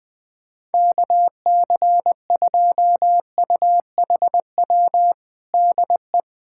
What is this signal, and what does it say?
Morse code, heard as a keyed tone, about 20 wpm, 700 Hz. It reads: KC2UHW DE
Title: Practice QSO - Exchange 4 - Ending - 20wpm